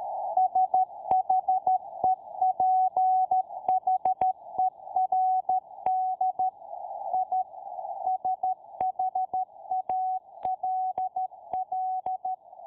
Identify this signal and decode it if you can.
Morse code, heard as a keyed tone, about 13 wpm, 750 Hz. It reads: SHEPHERD I SHALL